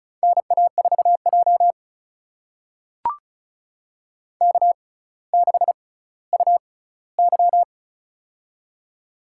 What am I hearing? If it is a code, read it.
Morse code, heard as a keyed tone, about 34 wpm, spaced as for 18 wpm, 700 Hz. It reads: NA4J K6UY
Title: USA Call Signs - 4 Characters - 6x Character Spacing - 35wpm